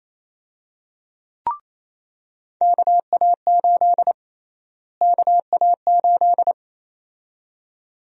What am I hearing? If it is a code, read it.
Morse code, heard as a keyed tone, about 28 wpm, 700 Hz. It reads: KA8 KA8
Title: USA Call Sign Prefixes - Repeated 2x - 28wpm